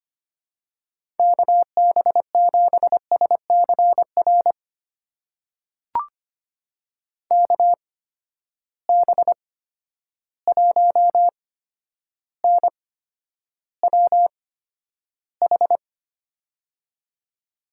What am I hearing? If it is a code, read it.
Morse code, heard as a keyed tone, about 25 wpm, spaced as for 11 wpm, 700 Hz. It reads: KB7SCR KB1NWH
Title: USA Call Signs - 6 Characters - 8x Character Spacing - 25wpm